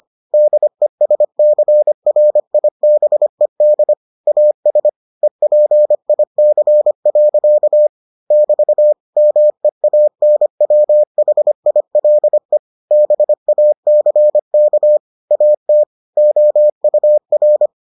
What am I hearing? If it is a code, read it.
Morse code, heard as a keyed tone, about 25 wpm, 600 Hz. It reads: DESCRIBED AS EPIC. <BT> MEANWHILE BACK AT OUR